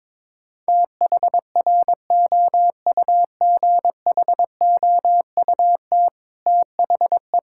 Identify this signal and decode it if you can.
Morse code, heard as a keyed tone, about 22 wpm, 700 Hz. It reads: THROUGHOUT THE
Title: Sentences from Top 1000 Words - Review Part 5 - 22wpm